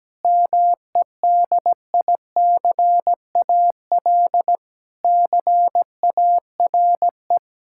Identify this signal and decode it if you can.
Morse code, heard as a keyed tone, about 17 wpm, 700 Hz. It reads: MEDICAL CARE
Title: Sentences from Top 1000 Words - Review Part 5 - 17wpm